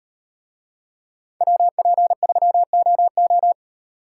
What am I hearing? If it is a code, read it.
Morse code, heard as a keyed tone, about 38 wpm, 700 Hz. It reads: WP3OO